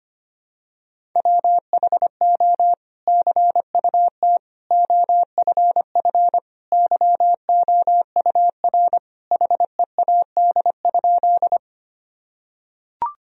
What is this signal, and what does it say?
Morse code, heard as a keyed tone, about 25 wpm, 700 Hz. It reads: WHO CUT OFF YOUR HEAD?
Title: Sentences from Top 1000 Words - Review Part 1 - 25wpm